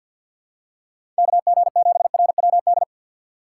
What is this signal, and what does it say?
Morse code, heard as a keyed tone, about 53 wpm, 700 Hz. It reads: KG7RWD
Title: USA Call Signs - 6 Characters - 6x Character Spacing - 50wpm